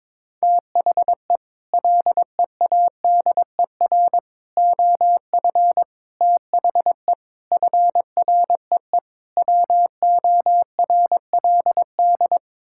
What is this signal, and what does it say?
Morse code, heard as a keyed tone, about 22 wpm, 700 Hz. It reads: THE LEADER OF THE FREE WORLD
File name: practice / 22wpm / Sets of 6 Words - 22wpm